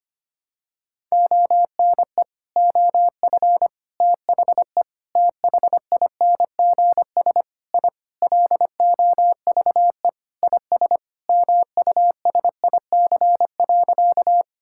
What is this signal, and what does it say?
Morse code, heard as a keyed tone, about 25 wpm, 700 Hz. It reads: ONE OF THE THINGS I LOVE IS MUSIC.
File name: practice / 25wpm / Sentences with 3-Word Phrases - 25wpm